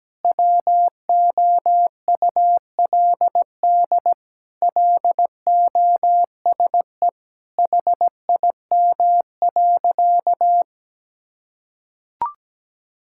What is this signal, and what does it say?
Morse code, heard as a keyed tone, about 17 wpm, 700 Hz. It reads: WOULD LOSE HIM.
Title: Sentences from Top 1600 Words 17wpm